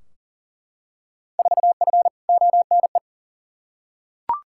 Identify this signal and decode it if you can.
Morse code, heard as a keyed tone, about 40 wpm, 700 Hz. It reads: 4F ONE
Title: Field Day Exchanges Encore - 40wpm